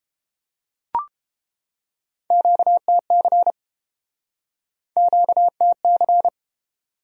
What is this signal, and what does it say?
Morse code, heard as a keyed tone, about 31 wpm, 700 Hz. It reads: QTC QTC